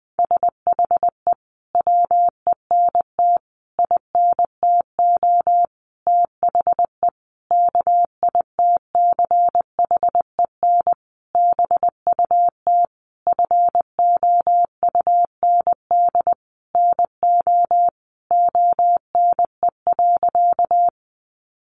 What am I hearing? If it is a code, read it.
Morse code, heard as a keyed tone, about 20 wpm, 700 Hz. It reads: SHE WENT INTO THE KITCHEN BUT FOUND NO ONE.
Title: Sentences from Top 1400 Words 20wpm